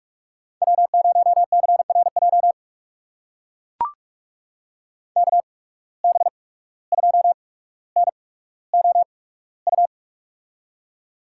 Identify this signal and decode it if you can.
Morse code, heard as a keyed tone, about 45 wpm, spaced as for 18 wpm, 700 Hz. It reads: W0CRJ KB2NOU
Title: USA Call Signs - 8x Character Spacing - 45wpm